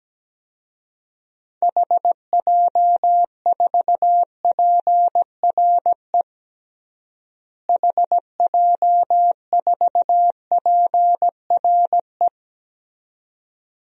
Morse code, heard as a keyed tone, about 17 wpm, 700 Hz. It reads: HJ4PRE HJ4PRE
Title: DXCC-100 Call Signs - 6 Characters - Repeated 2x - 17wpm